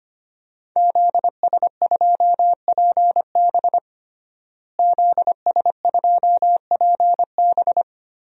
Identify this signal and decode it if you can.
Morse code, heard as a keyed tone, about 25 wpm, 700 Hz. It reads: ZS2PB ZS2PB
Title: DXCC-100 Call Signs - 5 Characters - Repeated 2x - 25wpm